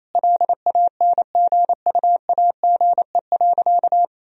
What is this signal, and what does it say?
Morse code, heard as a keyed tone, about 28 wpm, 700 Hz. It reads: LANGUAGE.
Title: Sentences from Top 1000 Words - Review Part 3 - 28wpm